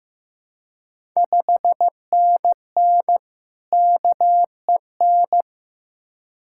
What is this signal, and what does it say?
Morse code, heard as a keyed tone, about 15 wpm, 690 Hz. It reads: 5NN KEN
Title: Ontario QSO Party Exchange 15wpm